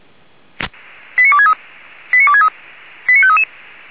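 Five-tone selective call (ZVEI), heard as quick runs of five tones, tones of about 70 ms. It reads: 89263 89263 89533
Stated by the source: ZVEI five-tone call 89347 received by SDR.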